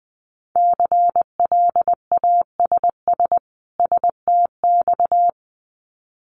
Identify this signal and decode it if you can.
Morse code, heard as a keyed tone, about 20 wpm, 700 Hz. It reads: CLASS STX
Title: Winter Field Day Exchanges 20wpm